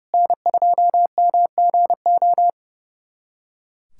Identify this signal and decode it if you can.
Morse code, heard as a keyed tone, about 30 wpm, 700 Hz. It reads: N2MGO
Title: SOTA QSO Practice 30wpm